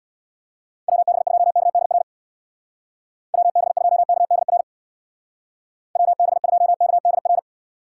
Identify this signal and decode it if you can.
Morse code, heard as a keyed tone, about 61 wpm, 690 Hz. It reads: WB2KDR WB2KDR WB2KDR
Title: Speed Racing - USA 2x3 Call Signs 50wpm